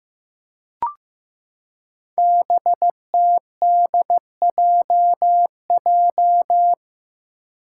Morse code, heard as a keyed tone, about 15 wpm, 700 Hz. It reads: BTDJJ
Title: Five Challenging Letter-Number Combinations 15wpm